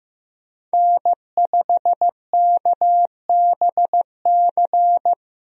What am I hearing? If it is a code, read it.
Morse code, heard as a keyed tone, about 15 wpm, 700 Hz. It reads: N5KBC
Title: USA Call Signs - 5 Characters - Repeated 3x - 15wpm